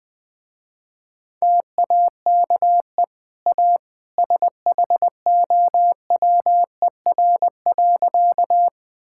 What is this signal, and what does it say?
Morse code, heard as a keyed tone, about 19 wpm, 700 Hz. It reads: TAKE A SHOWER.
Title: Easy Sentences with Hints - 20wpm